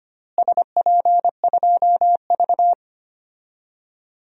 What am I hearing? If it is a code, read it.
Morse code, heard as a keyed tone, about 25 wpm, 700 Hz. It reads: SP2V
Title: DXCC-100 Call Signs - 4 Characters - Repeated 3x - 25wpm